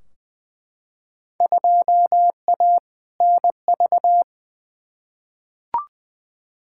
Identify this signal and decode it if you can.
Morse code, heard as a keyed tone, about 20 wpm, 700 Hz. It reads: 2A NV
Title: Field Day Exchanges - 20wpm